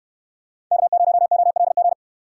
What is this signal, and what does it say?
Morse code, heard as a keyed tone, about 63 wpm, 700 Hz. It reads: K0OLG